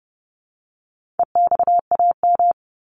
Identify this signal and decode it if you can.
Morse code, heard as a keyed tone, about 30 wpm, 700 Hz. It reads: EXAM
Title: Top 501-750 Words 30wpm